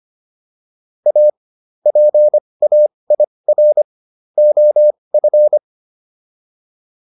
Morse code, heard as a keyed tone, about 25 wpm, spaced as for 20 wpm, 600 Hz. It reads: A PAIR OF
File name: WD202 20